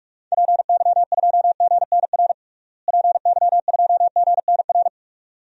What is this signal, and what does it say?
Morse code, heard as a keyed tone, about 44 wpm, 700 Hz. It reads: PY2GNR PY2GNR
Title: DXCC-100 Call Signs - 6 Characters - Repeated 2x - 45wpm